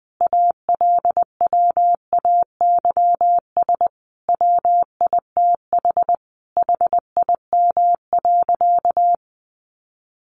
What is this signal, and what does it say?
Morse code, heard as a keyed tone, about 20 wpm, 700 Hz. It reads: ALWAYS WITH HIM.
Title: Sentences from Top 400 Words 20wpm